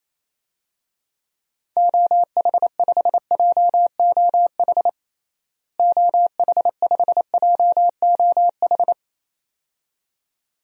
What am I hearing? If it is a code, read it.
Morse code, heard as a keyed tone, about 27 wpm, 700 Hz. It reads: OH5JOH OH5JOH
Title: DXCC-100 Call Signs - 6 Characters - Repeated 2x - 28wpm